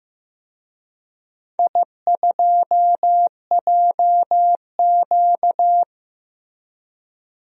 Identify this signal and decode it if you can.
Morse code, heard as a keyed tone, about 15 wpm, 700 Hz. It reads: I2JQ
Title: DXCC-100 Call Signs - 4 Characters - 15wpm